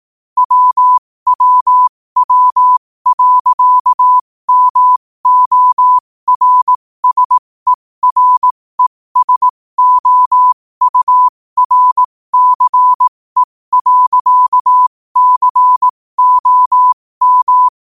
Morse code, heard as a keyed tone, about 18 wpm, spaced as for 15 wpm, 1000 Hz. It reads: WWW.MORSERESOURCE.COM
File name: qotdfiles / 15wpm